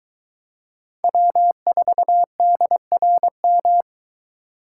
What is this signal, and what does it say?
Morse code, heard as a keyed tone, about 22 wpm, 700 Hz. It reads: W4DRM